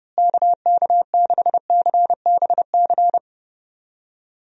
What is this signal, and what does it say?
Morse code, heard as a keyed tone, about 30 wpm, 700 Hz. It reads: KK6CBC
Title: USA Call Signs - 6 Characters - Repeated 3x - 30wpm